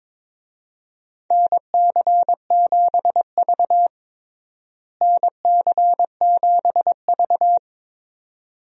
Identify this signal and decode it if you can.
Morse code, heard as a keyed tone, about 22 wpm, 700 Hz. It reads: NC7V NC7V
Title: USA Call Signs - 4 Characters - Repeated 2x - 22wpm